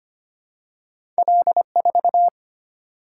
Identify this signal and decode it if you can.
Morse code, heard as a keyed tone, about 25 wpm, 690 Hz. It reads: L4
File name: Two Letter-Number Combinations - 25wpm